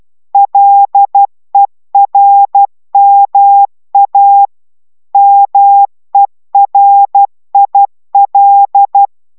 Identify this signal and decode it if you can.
Morse code, heard as a keyed tone, about 12 wpm, 800 Hz. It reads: LERMA MERIL